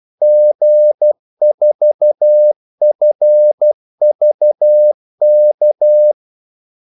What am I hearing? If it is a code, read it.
Morse code, heard as a keyed tone, about 12 wpm, 600 Hz. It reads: G4FVK
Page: 2 ekstra ordmellemrum mellem kaldesignalerne